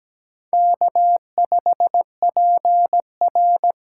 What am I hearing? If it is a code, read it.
Morse code, heard as a keyed tone, about 17 wpm, 700 Hz. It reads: K5PR
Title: USA Call Signs - 4 Characters - 2x Character Spacing - 17wpm